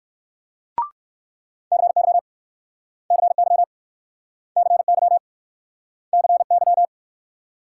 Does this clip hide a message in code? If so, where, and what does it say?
Yes: Morse code from 1.7 to 6.9 s, heard as a keyed tone, about 59 wpm, 700 Hz. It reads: CY CY CY CY